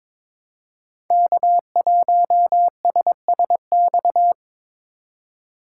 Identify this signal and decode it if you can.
Morse code, heard as a keyed tone, about 22 wpm, 700 Hz. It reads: K1SSX
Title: Code Groups - 5 Letter:Numbers - 22wpm